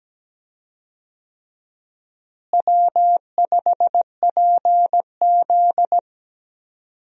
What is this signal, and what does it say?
Morse code, heard as a keyed tone, about 17 wpm, 700 Hz. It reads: W5PZ
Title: USA Call Signs - 4 Characters - Repeated 3x - 17wpm